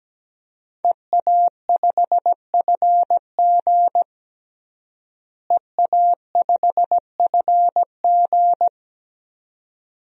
Morse code, heard as a keyed tone, about 17 wpm, 700 Hz. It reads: EA5FG EA5FG
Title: DXCC-100 Call Signs - 5 Characters - Repeated 2x - 17wpm